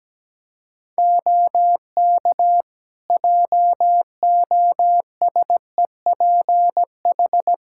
Morse code, heard as a keyed tone, about 17 wpm, 700 Hz. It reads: OK JOSEPH
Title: Practice QSO - Exchange 3 - More Info - 17wpm